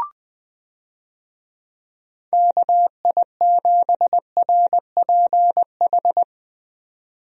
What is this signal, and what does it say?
Morse code, heard as a keyed tone, about 20 wpm, 700 Hz. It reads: KI7RPH
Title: USA Call Signs - 6 Characters - 20wpm